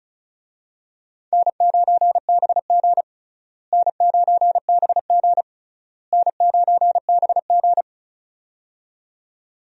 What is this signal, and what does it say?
Morse code, heard as a keyed tone, about 36 wpm, 700 Hz. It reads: N9BG N9BG N9BG
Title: USA Call Signs - 4 Characters - Repeated 3x - 35wpm